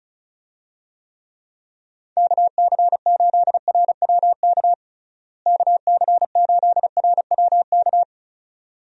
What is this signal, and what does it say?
Morse code, heard as a keyed tone, about 36 wpm, 700 Hz. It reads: KC8RWK KC8RWK
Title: USA Call Signs - 6 Characters - Repeated 2x - 35wpm